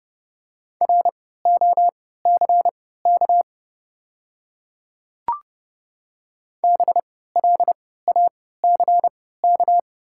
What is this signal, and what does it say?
Morse code, heard as a keyed tone, about 31 wpm, spaced as for 17 wpm, 700 Hz. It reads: ROCK BLACK
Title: 3x Character Spacing - Top 500 Words - 30wpm